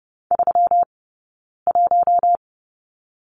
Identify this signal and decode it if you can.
Morse code, heard as a keyed tone, about 30 wpm, spaced as for 9 wpm, 700 Hz. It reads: 31
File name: Lesson 20d - Number 3 - Two Characters 30wpm